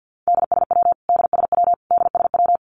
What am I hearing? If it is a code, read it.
Morse code, heard as a keyed tone, about 50 wpm, 700 Hz. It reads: DSW DSW DSW
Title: Abbreviations Repeated 3x - 50wpm